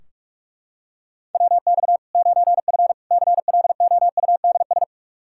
Morse code, heard as a keyed tone, about 44 wpm, 700 Hz. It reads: WX 9F CLOUDS
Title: Practice QSO - Exchange 2 - Info - 45wpm